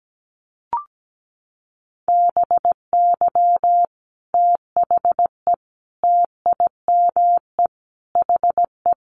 Morse code, heard as a keyed tone, about 17 wpm, 690 Hz. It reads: BY THE TIME HE
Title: Sets of 5 Words Encore 17wpm